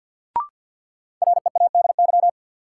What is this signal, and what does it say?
Morse code, heard as a keyed tone, about 51 wpm, 700 Hz. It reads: READY
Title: Sentences from Top 400 Words - New Word Drill - 50wpm